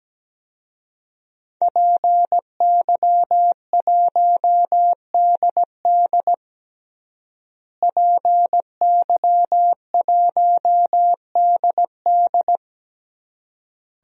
Morse code, heard as a keyed tone, about 17 wpm, 700 Hz. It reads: PY1DD PY1DD
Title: DXCC-100 Call Signs - 5 Characters - Repeated 3x - 17wpm